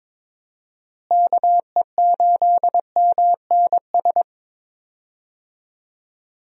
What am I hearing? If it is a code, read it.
Morse code, heard as a keyed tone, about 21 wpm, 700 Hz. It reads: KE8MNS
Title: USA Call Signs - 6 Characters - Repeated 3x - 22wpm